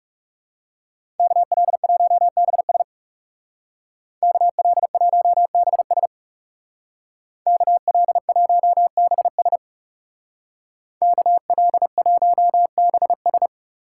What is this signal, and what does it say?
Morse code, heard as a keyed tone, about 37 wpm, 700 Hz. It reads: KL1BS KL1BS KL1BS KL1BS
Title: Speed Racing - USA 2x2 Call Signs 30wpm